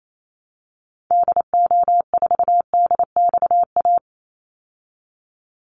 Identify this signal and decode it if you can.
Morse code, heard as a keyed tone, about 28 wpm, 700 Hz. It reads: DO4DXA
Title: Call Signs Repeated 3x - 28wpm